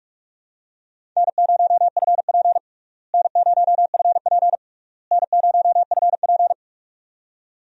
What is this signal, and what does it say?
Morse code, heard as a keyed tone, about 46 wpm, 700 Hz. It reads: N0FP N0FP N0FP
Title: USA Call Signs - 4 Characters - Repeated 3x - 45wpm